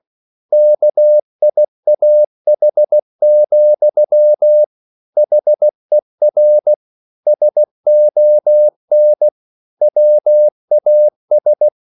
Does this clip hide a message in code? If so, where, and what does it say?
Yes: Morse code from 0.5 to 11.7 s, heard as a keyed tone, about 16 wpm, 600 Hz. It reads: KIAH, HER SON WAS